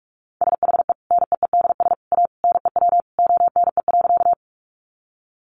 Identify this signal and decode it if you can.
Morse code, heard as a keyed tone, about 44 wpm, 700 Hz. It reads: SHE NEEDS A NEW ONE.
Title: Additional Easy Phrases 45wpm